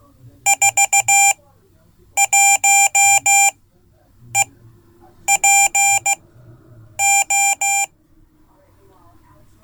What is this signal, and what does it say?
Morse code, heard as a keyed tone, about 16 wpm, spaced as for 8 wpm, 790 Hz. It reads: 41EPO
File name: Code / 5min-c